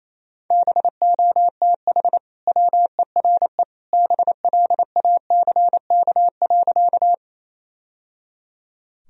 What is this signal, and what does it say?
Morse code, heard as a keyed tone, about 28 wpm, 700 Hz. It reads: BOTH WERE BLACK.
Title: Sentences from Top 1000 Words - Review Part 2 - 28wpm